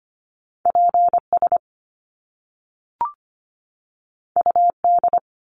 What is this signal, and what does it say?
Morse code, heard as a keyed tone, about 25 wpm, 700 Hz. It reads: PS UD
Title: Top Two Letter Combinations in Words 25wpm